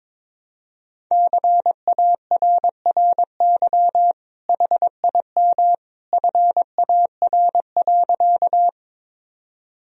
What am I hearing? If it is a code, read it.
Morse code, heard as a keyed tone, about 22 wpm, 700 Hz. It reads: CARRY HIM FAR.
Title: Sentences from Top 1000 Words - Review Part 1 - 22wpm